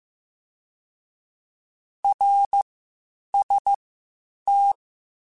Morse code, heard as a keyed tone, about 15 wpm, spaced as for 8 wpm, 800 Hz. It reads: RST
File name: (spaced) / QSOs Spaced-mp3s / kc7haa @13 sp